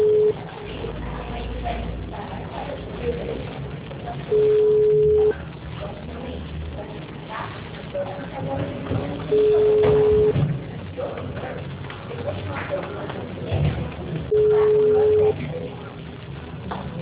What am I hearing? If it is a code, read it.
Ringback tone (Europe, 425 Hz).
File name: out- -2008- - - .30929